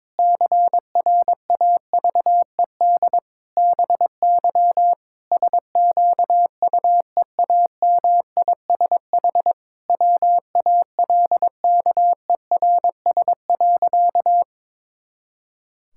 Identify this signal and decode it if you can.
Morse code, heard as a keyed tone, about 22 wpm, 700 Hz. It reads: CRAVED BY SQUEAMISH WALKERS.
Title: Pangram Challenge v2 - 22wpm